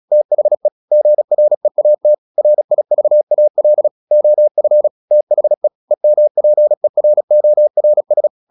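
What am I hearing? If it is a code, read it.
Morse code, heard as a keyed tone, about 36 wpm, 600 Hz. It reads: THE GREAT RIVAL OF THE EMPERORS
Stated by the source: Morse practice at 36 wpm, text of Ken Follett's novel Never, from CW med Gnister.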